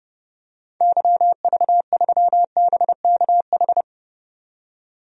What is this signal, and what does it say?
Morse code, heard as a keyed tone, about 30 wpm, 700 Hz. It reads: YV3BKH
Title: DXCC-100 Call Signs - 6 Characters - 30wpm